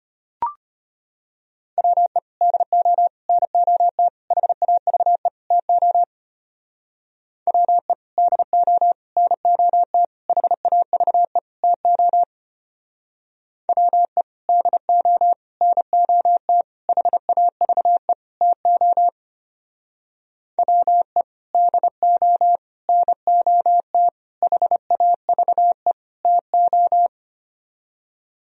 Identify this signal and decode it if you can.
Morse code, heard as a keyed tone, about 32 wpm, 700 Hz. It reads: WE DO NOT HAVE TO WE DO NOT HAVE TO WE DO NOT HAVE TO WE DO NOT HAVE TO